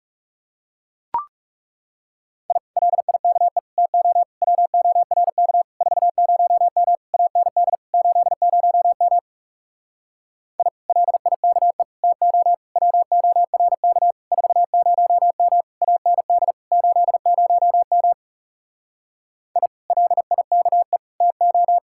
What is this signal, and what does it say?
Morse code, heard as a keyed tone, about 37 wpm, 700 Hz. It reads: I LIKE TO WORK 40M AND 80M I LIKE TO WORK 40M AND 80M I LIKE TO
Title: Speed-Racing Common QSO Phrases - 30wpm